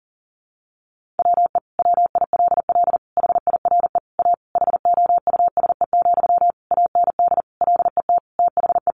Morse code, heard as a keyed tone, about 37 wpm, 700 Hz. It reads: WE WILL HIRE A HOUSE, AND LET THE
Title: Sentences from Top 1400 Words 40wpm